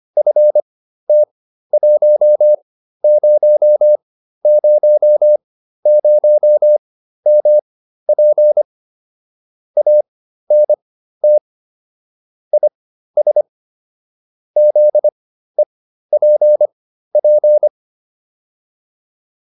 Morse code, heard as a keyed tone, about 25 wpm, spaced as for 13 wpm, 600 Hz. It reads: FT1000MP ANT IS ZEPP
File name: QSO401 13